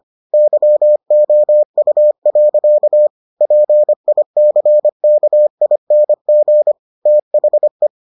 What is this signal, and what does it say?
Morse code, heard as a keyed tone, about 25 wpm, 600 Hz. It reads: YOU. PICKING THE